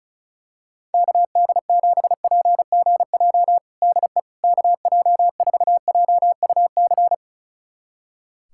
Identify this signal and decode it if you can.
Morse code, heard as a keyed tone, about 36 wpm, 700 Hz. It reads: KD7PGJ DE KJ4JUC
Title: USA Callsign DE Callsign 35wpm